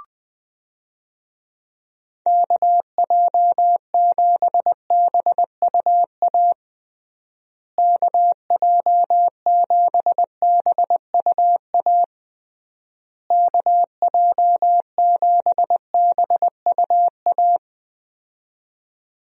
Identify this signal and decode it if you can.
Morse code, heard as a keyed tone, about 20 wpm, 700 Hz. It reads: KJ7BUA KJ7BUA KJ7BUA
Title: USA Call Signs - 6 Characters - Repeated 3x - 20wpm